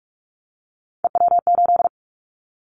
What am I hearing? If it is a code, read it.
Morse code, heard as a keyed tone, about 46 wpm, 700 Hz. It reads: EW8
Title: Callsign Prefixes for Top 50 DXCC - 45wpm